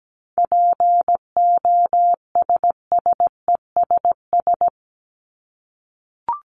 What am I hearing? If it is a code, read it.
Morse code, heard as a keyed tone, about 17 wpm, 700 Hz. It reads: POSSESS
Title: Top 1751-2000 Words 17wpm